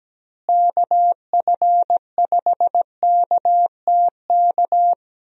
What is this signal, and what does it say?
Morse code, heard as a keyed tone, about 17 wpm, 700 Hz. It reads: KF5KTK
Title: USA Call Signs - 6 Characters - Repeated 3x - 17wpm